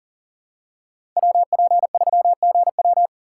Morse code, heard as a keyed tone, about 40 wpm, 700 Hz. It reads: WP3GW
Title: USA Call Signs - 5 Characters - 6x Character Spacing - 40wpm